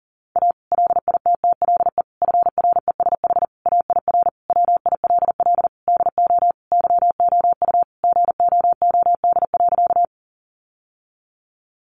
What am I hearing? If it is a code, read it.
Morse code, heard as a keyed tone, about 40 wpm, 700 Hz. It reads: A LITTLE FRESH AIR WILL DO YOU GOOD.
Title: Sentences from Top 1200 Words 40wpm